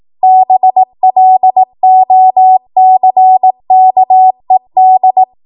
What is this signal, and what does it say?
Morse code, heard as a keyed tone, about 18 wpm, 750 Hz. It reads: BLOCKED
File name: Morse / 18WPM